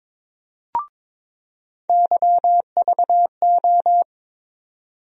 Morse code, heard as a keyed tone, about 22 wpm, 700 Hz. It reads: YVO
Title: USA Call Sign Postfixes - 3 Letters 22wpm